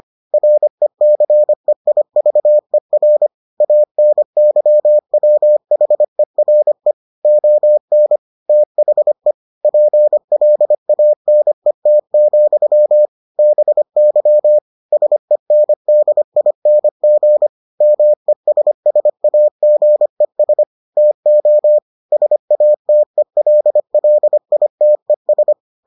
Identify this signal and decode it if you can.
Morse code, heard as a keyed tone, about 25 wpm, 600 Hz. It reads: RECEIVER ANYWHERE ON THE PLANET, BY SENDING MESSAGES TO SATELLITES